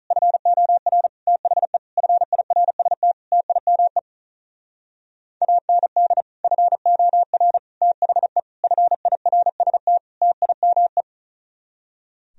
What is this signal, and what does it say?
Morse code, heard as a keyed tone, about 43 wpm, 700 Hz. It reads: FOR THE FIRST TIME AND FOR THE FIRST TIME
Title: Speed-Racing - Sets of 5 Words 35wpm